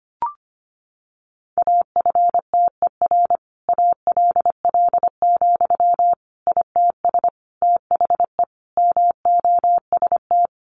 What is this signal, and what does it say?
Morse code, heard as a keyed tone, about 25 wpm, 700 Hz. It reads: AFTER ALL, ITS THE MOST
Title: Sentences from Top 700 Words 25wpm